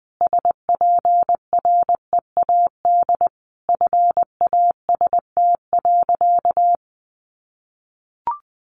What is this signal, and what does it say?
Morse code, heard as a keyed tone, about 20 wpm, 700 Hz. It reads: SPREAD FAST.
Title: Sentences from Top 1200 Words 20wpm